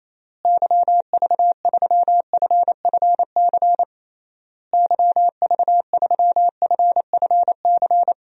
Morse code, heard as a keyed tone, about 27 wpm, 700 Hz. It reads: YV3FFC YV3FFC
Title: DXCC-100 Call Signs - 6 Characters - Repeated 3x - 28wpm